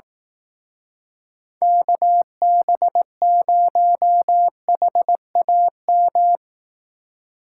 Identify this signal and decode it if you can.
Morse code, heard as a keyed tone, about 18 wpm, 700 Hz. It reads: KB0HAM
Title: Speed Racing - USA 2x3 Call Signs 15wpm